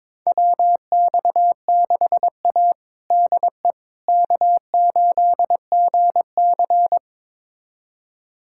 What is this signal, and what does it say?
Morse code, heard as a keyed tone, about 22 wpm, 700 Hz. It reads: WX6A DE K8GC
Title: USA Callsign DE Callsign - 4 Characters - 22wpm